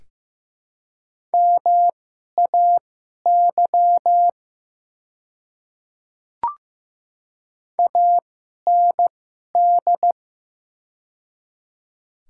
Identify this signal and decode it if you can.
Morse code, heard as a keyed tone, about 15 wpm, spaced as for 11 wpm, 700 Hz. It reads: MAY AND
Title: 2x Character Spacing - Top 100 Words - 15wpm